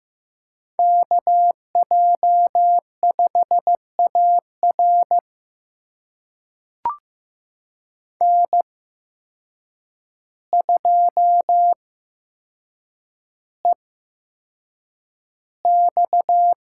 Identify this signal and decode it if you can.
Morse code, heard as a keyed tone, about 15 wpm, spaced as for 7 wpm, 700 Hz. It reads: KJ5AR N2EX
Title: USA Call Signs - 8x Character Spacing - 15wpm